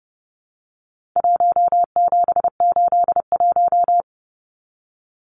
Morse code, heard as a keyed tone, about 30 wpm, 700 Hz. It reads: 1781
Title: Numbers - Contest Style - 4-Digit 30wpm